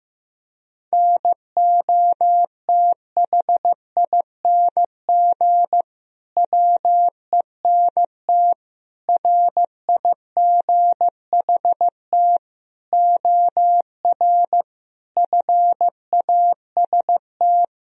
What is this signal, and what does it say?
Morse code, heard as a keyed tone, about 15 wpm, 700 Hz. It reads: NOTHING WENT RIGHT OR FAST
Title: Sentences from Top 1000 Words - Review Part 2 - 15wpm